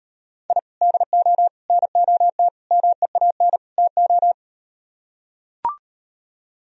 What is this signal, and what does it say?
Morse code, heard as a keyed tone, about 38 wpm, 700 Hz. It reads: I DO NOT MEAN TO